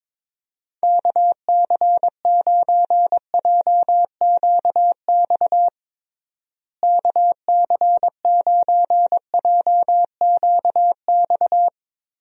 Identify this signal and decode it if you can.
Morse code, heard as a keyed tone, about 22 wpm, 700 Hz. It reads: KC9JQX KC9JQX
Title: USA Call Signs - 6 Characters - Repeated 3x - 22wpm